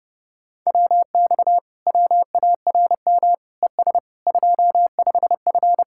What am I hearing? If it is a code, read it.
Morse code, heard as a keyed tone, about 30 wpm, 700 Hz. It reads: WX WARM ES 25F